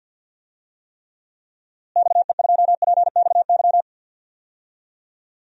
Morse code, heard as a keyed tone, about 52 wpm, 700 Hz. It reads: XE2PXY
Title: DXCC-100 Call Signs - 6 Characters - Repeated 3x - 50wpm